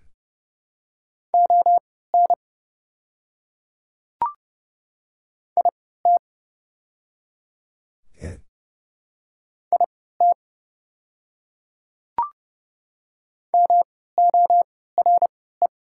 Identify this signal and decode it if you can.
Morse code, heard as a keyed tone, about 30 wpm, spaced as for 17 wpm, 700 Hz. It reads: ON IT IT MORE
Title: 3x Character Spacing - Top 100 Words - 30wpm